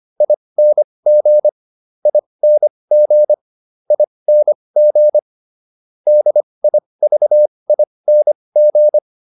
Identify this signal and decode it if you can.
Morse code, heard as a keyed tone, about 25 wpm, spaced as for 20 wpm, 600 Hz. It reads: ING ING ING DIVING